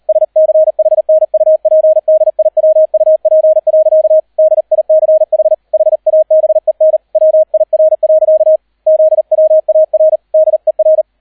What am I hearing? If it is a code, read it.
Morse code, heard as a keyed tone, about 38 wpm, 630 Hz, with radio noise. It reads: SCHNUPDIWUP. DICH HABEN WIR. ZWAR DER